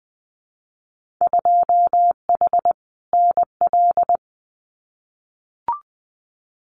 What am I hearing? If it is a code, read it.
Morse code, heard as a keyed tone, about 19 wpm, 700 Hz. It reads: 2H NL
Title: Winter Field Day Exchanges Encore 20wpm